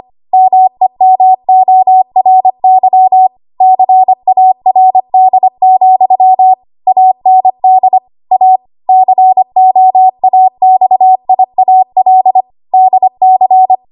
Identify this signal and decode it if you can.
Morse code, heard as a keyed tone, about 25 wpm, 750 Hz. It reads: MEMORY CARD, AND A COAXIAL DC
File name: Archive / 25 WPM / 25WPM